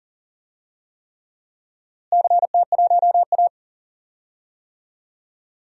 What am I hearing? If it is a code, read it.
Morse code, heard as a keyed tone, about 37 wpm, 700 Hz. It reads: CT1A
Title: DXCC-100 Call Signs - 4 Characters - Repeated 3x - 40wpm